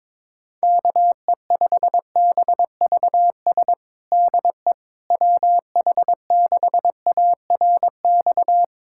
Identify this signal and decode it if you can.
Morse code, heard as a keyed tone, about 22 wpm, 700 Hz. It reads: KE5BVS DE WH6ARX
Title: USA Callsign DE Callsign - 6 Characters - 22wpm